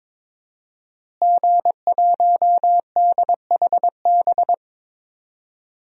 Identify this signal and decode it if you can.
Morse code, heard as a keyed tone, about 22 wpm, 700 Hz. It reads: G1DHB
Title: DXCC-100 Call Signs - 5 Characters - Repeated 3x - 22wpm